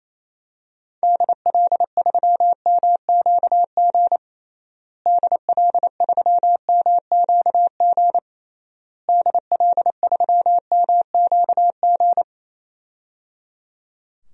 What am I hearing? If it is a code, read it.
Morse code, heard as a keyed tone, about 28 wpm, 700 Hz. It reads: DL3MQG DL3MQG DL3MQG
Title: DXCC-100 Call Signs - 6 Characters - Repeated 3x - 28wpm